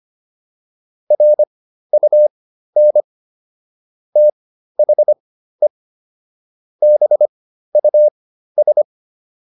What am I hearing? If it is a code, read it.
Morse code, heard as a keyed tone, about 25 wpm, spaced as for 13 wpm, 600 Hz. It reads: RUN THE BUS